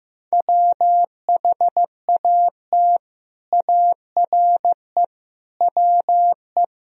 Morse code, heard as a keyed tone, about 15 wpm, 700 Hz. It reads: WHAT ARE WE
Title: Speed-Racing - Sets of 5 Words 15wpm